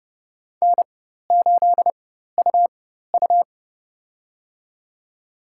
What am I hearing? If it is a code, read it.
Morse code, heard as a keyed tone, about 30 wpm, spaced as for 14 wpm, 700 Hz. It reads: N8UU
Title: USA Call Signs - 4 Characters - 4x Character Spacing - 30wpm